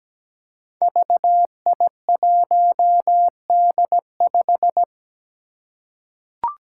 Morse code, heard as a keyed tone, about 17 wpm, 700 Hz. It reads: VI1D5
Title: Five Challenging Letter-Number Combinations 17wpm